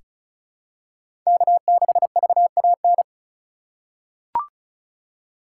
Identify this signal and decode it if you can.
Morse code, heard as a keyed tone, about 36 wpm, 700 Hz. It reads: K6VAN
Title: USA Call Signs - 5 Characters - 2x Character Spacing - 35wpm